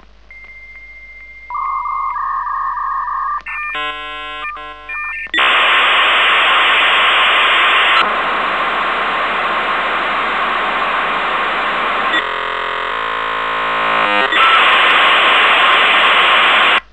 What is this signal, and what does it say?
Dial-up modem handshake.